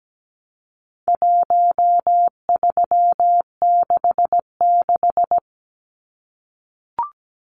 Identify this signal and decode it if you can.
Morse code, heard as a keyed tone, about 17 wpm, 700 Hz. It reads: 1366
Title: Numbers - Contest Style - 4-Digit 17wpm